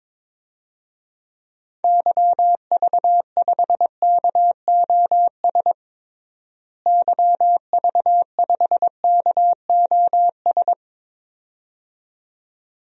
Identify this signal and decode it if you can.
Morse code, heard as a keyed tone, about 22 wpm, 700 Hz. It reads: YV5KOS YV5KOS
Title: DXCC-100 Call Signs - 6 Characters - Repeated 2x - 22wpm